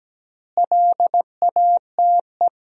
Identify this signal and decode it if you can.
Morse code, heard as a keyed tone, about 17 wpm, 700 Hz. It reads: LATE
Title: 3x Word-Spacing - Sentences from Top 300 Words - 17wpm